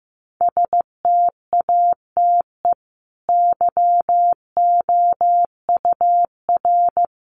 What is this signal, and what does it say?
Morse code, heard as a keyed tone, about 15 wpm, 700 Hz. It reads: STATE YOUR
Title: Sentences from Top 700 Words 15wpm